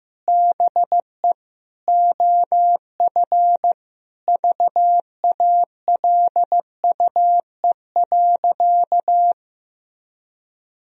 Morse code, heard as a keyed tone, about 15 wpm, 700 Hz. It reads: BE OF VALUE.
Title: Sentences from Top 1000 Words - Review Part 3 - 15wpm